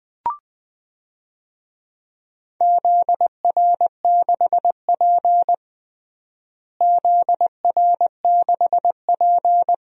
Morse code, heard as a keyed tone, about 20 wpm, 700 Hz. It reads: ZR6P ZR6P
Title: DXCC-100 Call Signs - 4 Characters - Repeated 3x - 20wpm